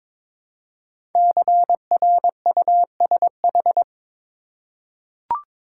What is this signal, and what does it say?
Morse code, heard as a keyed tone, about 22 wpm, 700 Hz. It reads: CRUSH
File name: 5 Letter Words - First 3 Letters Given - 22wpm